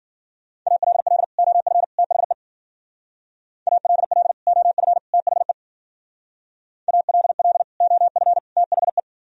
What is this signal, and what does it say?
Morse code, heard as a keyed tone, about 55 wpm, 700 Hz. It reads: ALL OF THE ALL OF THE ALL OF THE